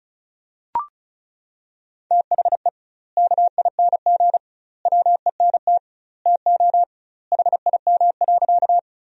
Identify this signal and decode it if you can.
Morse code, heard as a keyed tone, about 36 wpm, 700 Hz. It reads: THE KING WENT TO HIM.
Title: 2x Word-Spacing - Sentences from Top 400 Words - 35wpm